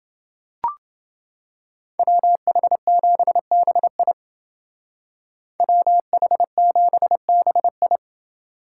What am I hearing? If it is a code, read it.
Morse code, heard as a keyed tone, about 28 wpm, 700 Hz. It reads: WH7BI WH7BI